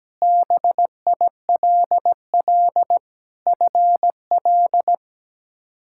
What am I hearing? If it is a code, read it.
Morse code, heard as a keyed tone, about 17 wpm, 700 Hz. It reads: BILL FL
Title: K1USN SST QSO Practice - 17wpm